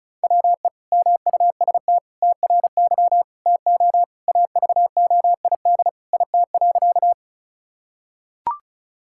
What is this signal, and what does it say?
Morse code, heard as a keyed tone, about 34 wpm, 700 Hz. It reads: WE MUST TRY TO AVOID IT.
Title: Sentences from Top 1000 Words - Review Part 6 - 35wpm